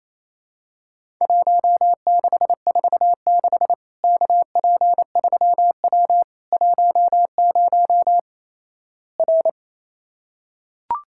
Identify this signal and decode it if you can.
Morse code, heard as a keyed tone, about 28 wpm, 700 Hz. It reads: 1646 KP3W 10 R
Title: Worked All Europe DX Contest - QTC Exchange - 28wpm